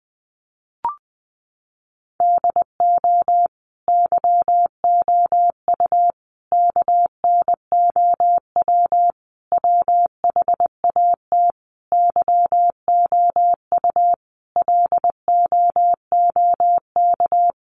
Morse code, heard as a keyed tone, about 20 wpm, 700 Hz. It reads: DO YOU KNOW WHAT YOU LOOK
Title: Sentences from Top 100 Words 20wpm